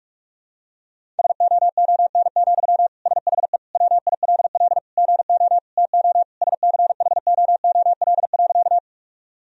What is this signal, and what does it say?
Morse code, heard as a keyed tone, about 46 wpm, 700 Hz. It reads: SOON, SHE WILL GO TO SCHOOL.
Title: Sentences from Top 1000 Words - Review Part 1 - 45wpm